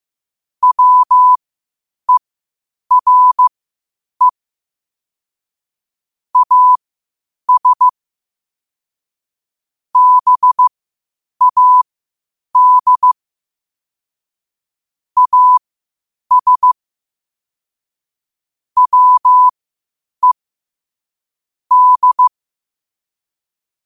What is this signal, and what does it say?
Morse code, heard as a keyed tone, about 15 wpm, spaced as for 8 wpm, 1000 Hz. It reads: WERE AS BAD AS WE D
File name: qotdfiles / 8wpm